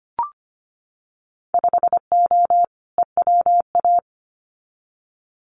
Morse code, heard as a keyed tone, about 25 wpm, 700 Hz. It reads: 5O EWA
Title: Winter Field Day Exchanges Encore 25wpm